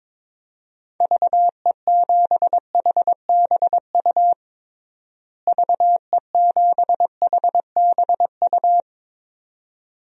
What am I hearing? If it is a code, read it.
Morse code, heard as a keyed tone, about 22 wpm, 700 Hz. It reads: VE7HBU VE7HBU
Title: DXCC-100 Call Signs - 6 Characters - Repeated 2x - 22wpm